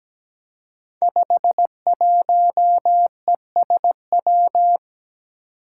Morse code, heard as a keyed tone, about 17 wpm, 700 Hz. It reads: 51ESW
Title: Five Challenging Letter-Number Combinations 17wpm